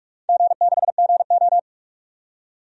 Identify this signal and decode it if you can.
Morse code, heard as a keyed tone, about 41 wpm, 700 Hz. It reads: G6GO
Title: DXCC-100 Call Signs - 4 Characters - 45wpm